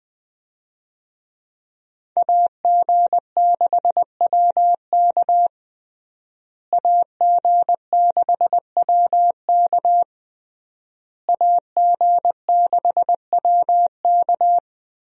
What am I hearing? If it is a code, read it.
Morse code, heard as a keyed tone, about 21 wpm, 700 Hz. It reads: AG6WK AG6WK AG6WK
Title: USA Call Signs - 5 Characters - Repeated 3x - 20wpm